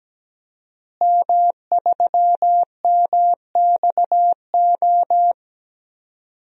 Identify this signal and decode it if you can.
Morse code, heard as a keyed tone, about 17 wpm, 700 Hz. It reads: M3MXO
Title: DXCC-100 Call Signs - 5 Characters - Repeated 2x - 17wpm